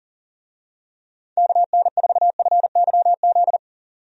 Morse code, heard as a keyed tone, about 39 wpm, 700 Hz. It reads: KN4FYZ
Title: USA Call Signs - 6 Characters - 40wpm